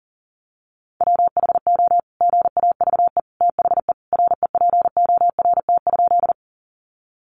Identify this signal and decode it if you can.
Morse code, heard as a keyed tone, about 40 wpm, 700 Hz. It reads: WHO GAVE THE REPORT?
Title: Sentences from Top 900 Words 40wpm